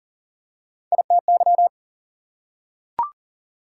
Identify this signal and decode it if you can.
Morse code, heard as a keyed tone, about 40 wpm, 700 Hz. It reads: ITY
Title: Prefixes and Suffixes 40wpm